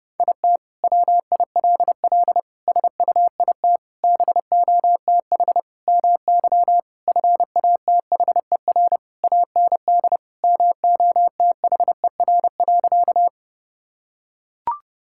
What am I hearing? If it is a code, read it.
Morse code, heard as a keyed tone, about 30 wpm, 700 Hz. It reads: IT WILL SUIT BOTH MY FATHER AND MOTHER.
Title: Sentences from Top 1600 Words 30wpm